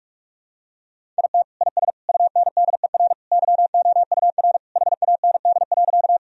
Morse code, heard as a keyed tone, about 46 wpm, 700 Hz. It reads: IT IS UNDER YOUR HAND.
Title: Sentences from Top 1000 Words - Review Part 1 - 45wpm